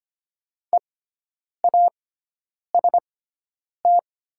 Morse code, heard as a keyed tone, about 25 wpm, spaced as for 9 wpm, 700 Hz. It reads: EAST
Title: 6x Character Spacing - Top 500 Words - 25wpm